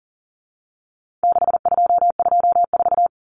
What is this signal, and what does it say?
Morse code, heard as a keyed tone, about 40 wpm, 700 Hz. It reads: 6224
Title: Common Cut Numbers - 4-Digit 40wpm